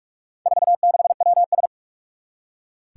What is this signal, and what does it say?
Morse code, heard as a keyed tone, about 46 wpm, 700 Hz. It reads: 46WS
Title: Four Challenging Letter-Number Combinations 45wpm